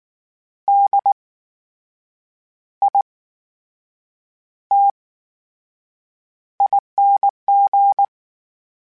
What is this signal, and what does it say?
Morse code, heard as a keyed tone, about 19 wpm, spaced as for 7 wpm, 800 Hz. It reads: DITING
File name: Special INGs / Special INGs-mp3s / VerySlowToFaster Ing @18